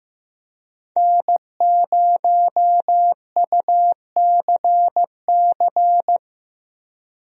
Morse code, heard as a keyed tone, about 15 wpm, 700 Hz. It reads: N0UCC
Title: USA Call Signs - 5 Characters - Repeated 2x - 15wpm